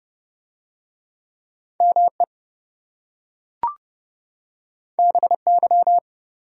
Morse code, heard as a keyed tone, about 27 wpm, 700 Hz. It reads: ME BY